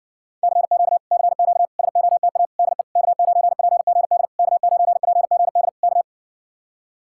Morse code, heard as a keyed tone, about 60 wpm, 700 Hz. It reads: CQ CQ SOTA DE K9PGD K9PGD K